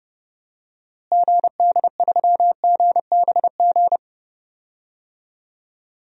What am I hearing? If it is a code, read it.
Morse code, heard as a keyed tone, about 32 wpm, 700 Hz. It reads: GD3GBG
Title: DXCC-100 Call Signs - 6 Characters - Repeated 3x - 30wpm